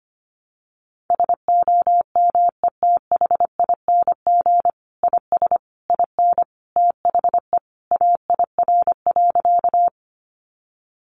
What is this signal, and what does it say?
Morse code, heard as a keyed tone, about 25 wpm, 700 Hz. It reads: SOMETHING IS IN THE AIR.
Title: Sentences from Top 600 Words 25wpm